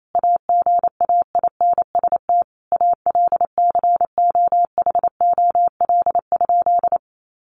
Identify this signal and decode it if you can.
Morse code, heard as a keyed tone, about 28 wpm, 700 Hz. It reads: AGAINST ALCOHOL?
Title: Sentences from Top 1400 Words 28wpm